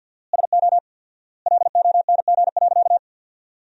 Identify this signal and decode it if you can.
Morse code, heard as a keyed tone, about 50 wpm, 700 Hz. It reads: SO LONG.
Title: 4x Word-Spacing - Sentences from Top 100 Words - 50wpm